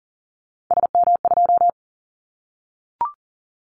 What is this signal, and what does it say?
Morse code, heard as a keyed tone, about 40 wpm, 700 Hz. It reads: SM2
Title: Callsign Prefixes for Top 25 DXCC - 40wpm